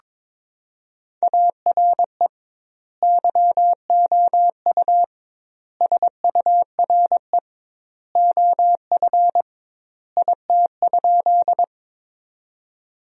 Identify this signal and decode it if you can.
Morse code, heard as a keyed tone, about 22 wpm, 700 Hz. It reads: ARE YOU SURE OF IT?
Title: 2x Word-Spacing - Sentences from Top 300 Words - 22wpm